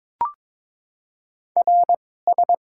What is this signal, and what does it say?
Morse code, heard as a keyed tone, about 22 wpm, spaced as for 16 wpm, 700 Hz. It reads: RS
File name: Top 90 QSO Elements - 2x Character Spacing - 22wpm